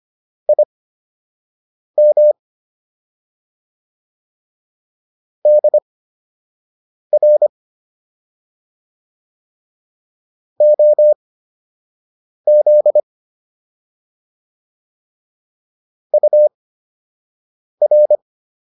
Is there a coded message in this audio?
Yes: Morse code from 0.5 to 18.2 s, heard as a keyed tone, about 25 wpm, spaced as for 6 wpm, 600 Hz. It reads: IM DR OZ UR